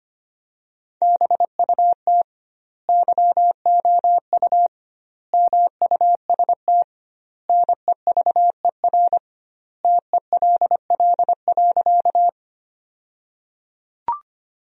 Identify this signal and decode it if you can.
Morse code, heard as a keyed tone, about 25 wpm, 700 Hz. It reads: BUT YOU MUST NEVER TELL.
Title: 2x Word-Spacing - Sentences from Top 300 Words - 25wpm